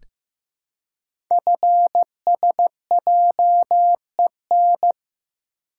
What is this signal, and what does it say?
Morse code, heard as a keyed tone, about 15 wpm, 700 Hz. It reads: FSJEN
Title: Code Groups - 5 Letters - 15wpm